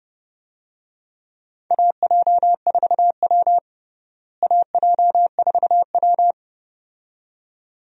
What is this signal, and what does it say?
Morse code, heard as a keyed tone, about 32 wpm, 700 Hz. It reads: AJ4W AJ4W
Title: USA Call Signs - 4 Characters - Repeated 2x - 30wpm